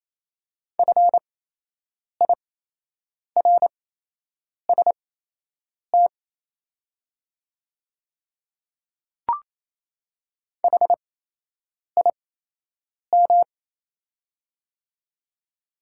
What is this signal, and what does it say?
Morse code, heard as a keyed tone, about 28 wpm, spaced as for 8 wpm, 690 Hz. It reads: FIRST HIM
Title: 8x Character Spacing - Top 200 Words - 28wpm